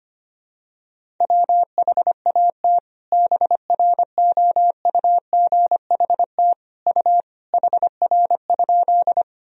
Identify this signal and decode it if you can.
Morse code, heard as a keyed tone, about 25 wpm, 700 Hz. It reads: WHAT BROUGHT U HR?
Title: Sentences from Top 500 Words and Abbreviations 25wpm